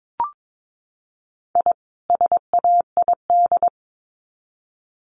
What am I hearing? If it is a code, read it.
Morse code, heard as a keyed tone, about 22 wpm, 700 Hz. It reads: I SAID
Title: Sets of 2 Words 22wpm